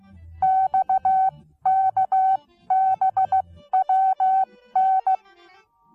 Morse code, heard as a keyed tone, about 15 wpm, spaced as for 13 wpm, 760 Hz. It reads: XKBWN
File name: Onlinelehrgang / morsen / DJ4UF Morsen19